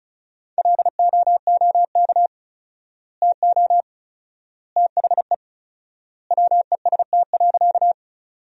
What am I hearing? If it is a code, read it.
Morse code, heard as a keyed tone, about 34 wpm, 700 Hz. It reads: LOOK TO THE WEST.
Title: 4x Word-Spacing - Sentences from Top 400 Words - 35wpm